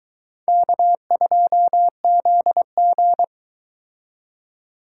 Morse code, heard as a keyed tone, about 22 wpm, 700 Hz. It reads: K2ZG